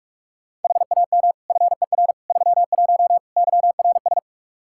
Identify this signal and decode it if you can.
Morse code, heard as a keyed tone, about 41 wpm, 700 Hz. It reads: HAM FER 31 YRS
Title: Practice QSO - Exchange 3 - More Info - 45wpm